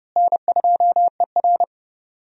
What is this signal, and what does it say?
Morse code, heard as a keyed tone, about 31 wpm, 700 Hz. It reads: N2ER
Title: USA Call Signs - 4 Characters - Repeated 3x - 30wpm